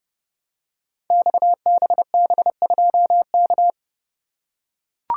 Morse code, heard as a keyed tone, about 30 wpm, 700 Hz. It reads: XBB2K
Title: Code Groups - 5 Letter:Numbers - 30wpm